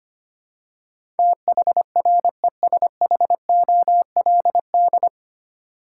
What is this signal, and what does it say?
Morse code, heard as a keyed tone, about 25 wpm, 700 Hz. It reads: THRESHOLD
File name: Nine Letter Words - 25wpm